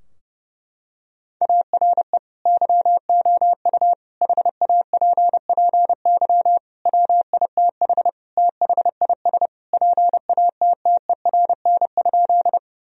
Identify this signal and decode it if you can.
Morse code, heard as a keyed tone, about 30 wpm, 700 Hz. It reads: ARE YOU HAPPY WITH THIS PATTERN?
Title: Sentences from Top 1000 Words - Review Part 2 - 30wpm